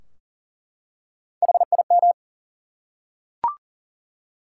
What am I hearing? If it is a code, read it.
Morse code, heard as a keyed tone, about 40 wpm, 700 Hz. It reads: HIM
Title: Three Letter Words - 40wpm